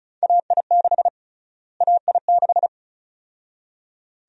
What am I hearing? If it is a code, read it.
Morse code, heard as a keyed tone, about 34 wpm, 700 Hz. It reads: AI6 AI6
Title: USA Call Sign Prefixes - Repeated 2x - 35wpm